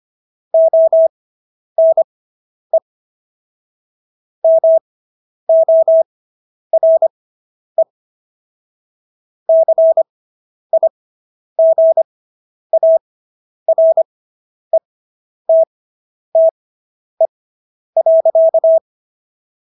Morse code, heard as a keyed tone, about 25 wpm, spaced as for 10 wpm, 650 Hz. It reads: ONE MORE CIGARETTE.